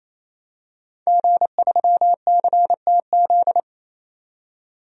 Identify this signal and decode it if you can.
Morse code, heard as a keyed tone, about 28 wpm, 700 Hz. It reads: G3CTZ
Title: DXCC-100 Call Signs - 5 Characters - 28wpm